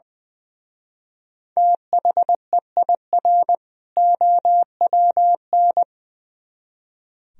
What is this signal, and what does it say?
Morse code, heard as a keyed tone, about 25 wpm, 700 Hz. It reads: THEIR OWN
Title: Speed-Racing - Sets of 2 Words 20wpm